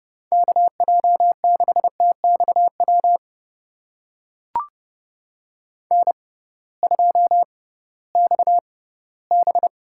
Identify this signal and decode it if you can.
Morse code, heard as a keyed tone, about 30 wpm, spaced as for 18 wpm, 700 Hz. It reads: KJ6TXW N2XB
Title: USA Call Signs - 6x Character Spacing - 30wpm